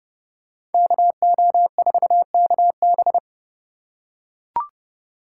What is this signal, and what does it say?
Morse code, heard as a keyed tone, about 29 wpm, 700 Hz. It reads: KO4KB